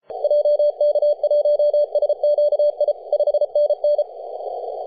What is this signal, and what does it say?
Morse code, heard as a keyed tone, about 33 wpm, 590 Hz. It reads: JK1SQI 5NN
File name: posts / 2015-july-18-0917z-s79hn-21-035mhz-cw